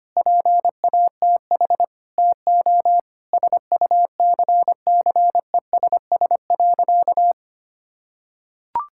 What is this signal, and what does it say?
Morse code, heard as a keyed tone, about 25 wpm, 700 Hz. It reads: PATH TO SUCCESS.
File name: practice / 25wpm / Difficult Sentences with Hints - 25wpm